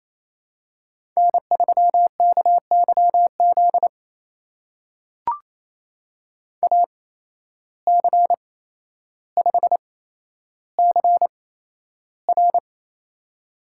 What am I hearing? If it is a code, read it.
Morse code, heard as a keyed tone, about 27 wpm, spaced as for 12 wpm, 700 Hz. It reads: N3KYZ AC5CR
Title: USA Call Signs - 5 Characters - 8x Character Spacing - 28wpm